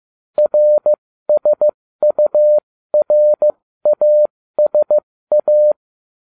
Morse code, heard as a keyed tone, about 15 wpm, spaced as for 13 wpm, 600 Hz. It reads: RSURASA